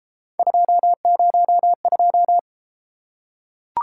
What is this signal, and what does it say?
Morse code, heard as a keyed tone, about 32 wpm, 700 Hz. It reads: 202